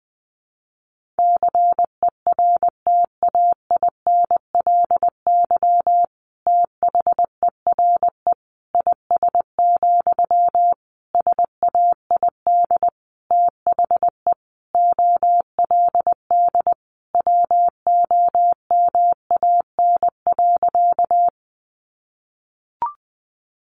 Morse code, heard as a keyed tone, about 20 wpm, 700 Hz. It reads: CERTAINLY THERE IS, SAID THE OLD WOMAN.
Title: Sentences from Top 1100 Words 20wpm